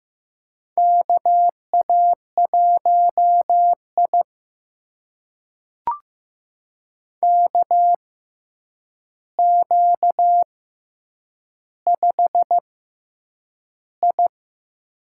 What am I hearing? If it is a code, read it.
Morse code, heard as a keyed tone, about 15 wpm, spaced as for 8 wpm, 700 Hz. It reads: KA1I KQ5I
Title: USA Call Signs - 4 Characters - 6x Character Spacing - 15wpm